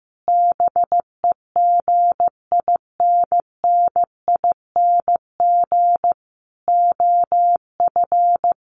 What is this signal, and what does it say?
Morse code, heard as a keyed tone, about 15 wpm, 700 Hz. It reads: BEGINNING OF
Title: Sentences from Top 900 Words 15wpm